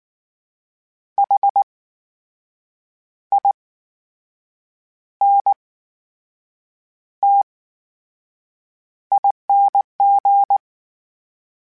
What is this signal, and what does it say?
Morse code, heard as a keyed tone, about 19 wpm, spaced as for 6 wpm, 800 Hz. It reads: HINTING
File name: Special INGs / Special INGs-mp3s / VerySlowToFaster Ing @18